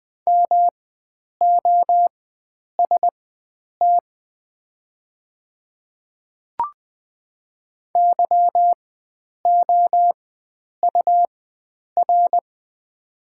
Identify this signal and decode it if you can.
Morse code, heard as a keyed tone, about 20 wpm, spaced as for 9 wpm, 690 Hz. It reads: MOST YOUR
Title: 4x Character Spacing - Top 100 Words - 20wpm